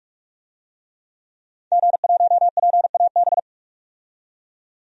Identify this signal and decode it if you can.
Morse code, heard as a keyed tone, about 44 wpm, 700 Hz. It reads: G1PAB
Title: DXCC-100 Call Signs - 5 Characters - 45wpm